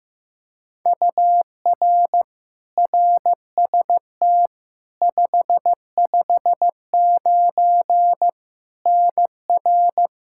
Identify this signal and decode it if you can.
Morse code, heard as a keyed tone, about 15 wpm, 700 Hz. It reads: UR RST 559 NR
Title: K1USN SST QSO Practice 15wpm